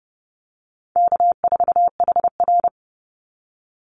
Morse code, heard as a keyed tone, about 30 wpm, 700 Hz. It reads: K4HR
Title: Easy Call Signs - 30wpm